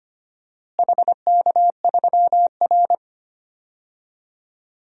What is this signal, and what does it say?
Morse code, heard as a keyed tone, about 25 wpm, 700 Hz. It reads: HK3R
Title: DXCC-100 Call Signs - 4 Characters - Repeated 3x - 25wpm